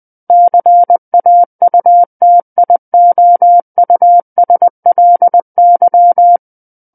Morse code, heard as a keyed tone, about 20 wpm, 700 Hz. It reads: CAUTIOUSLY
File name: 20 WPM